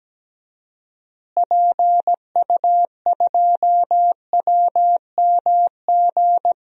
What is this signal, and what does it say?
Morse code, heard as a keyed tone, about 17 wpm, 700 Hz. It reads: PU2WMG
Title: DXCC-100 Call Signs - 6 Characters - Repeated 2x - 17wpm